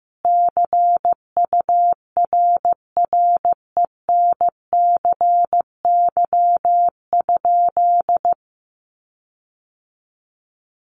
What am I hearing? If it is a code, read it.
Morse code, heard as a keyed tone, about 15 wpm, 700 Hz. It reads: CURRENCY?
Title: Sentences from Top 1500 Words 15wpm